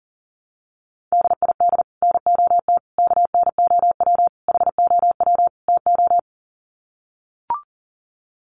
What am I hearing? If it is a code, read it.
Morse code, heard as a keyed tone, about 40 wpm, 700 Hz. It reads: DID NOT KNOW HOW TO
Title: Sets of 5 Words 40wpm